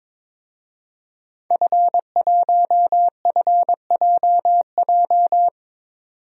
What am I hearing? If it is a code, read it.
Morse code, heard as a keyed tone, about 22 wpm, 700 Hz. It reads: F1FJJ
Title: DXCC-100 Call Signs - 5 Characters - 22wpm